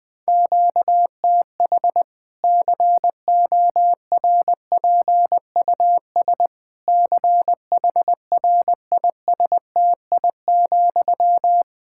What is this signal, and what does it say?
Morse code, heard as a keyed tone, about 20 wpm, 700 Hz. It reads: QTH CORPUS CHRISTI,
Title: Practice QSO - Exchange 1 - Beginning - 20wpm